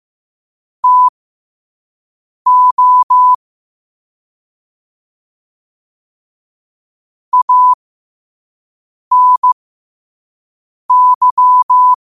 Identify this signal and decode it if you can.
Morse code, heard as a keyed tone, about 15 wpm, spaced as for 5 wpm, 1000 Hz. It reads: TO ANY
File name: qotdfiles / 5wpm